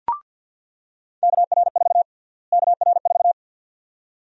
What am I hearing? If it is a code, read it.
Morse code, heard as a keyed tone, about 49 wpm, 700 Hz. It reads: KR4 KR4
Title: USA Call Sign Prefixes - Repeated 2x - 50wpm